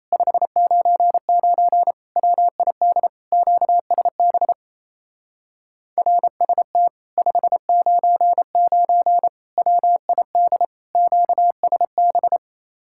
Morse code, heard as a keyed tone, about 34 wpm, 700 Hz. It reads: 599 WID QSB RST 599 WID QSB
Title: Speed-Racing Common QSO Phrases - 28wpm